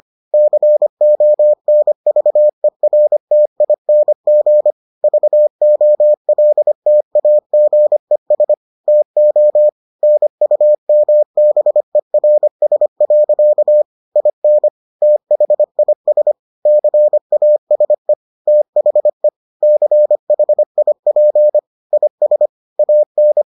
Morse code, heard as a keyed tone, about 25 wpm, 600 Hz. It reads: CONVERTING VOLTAGES TO NUMBERS. IN THIS CASE THE CHIP IS AN